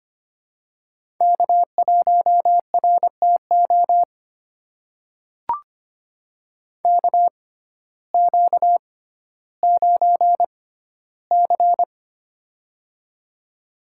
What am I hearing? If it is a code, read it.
Morse code, heard as a keyed tone, about 25 wpm, spaced as for 14 wpm, 700 Hz. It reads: K1RTO KQ9C
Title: USA Call Signs - 6x Character Spacing - 25wpm